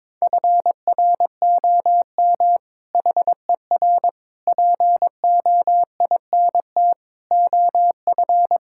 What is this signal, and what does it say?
Morse code, heard as a keyed tone, about 22 wpm, 700 Hz. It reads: FROM HER POINT OF
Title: Sentences from Top 1000 Words - Review Part 6 - 22wpm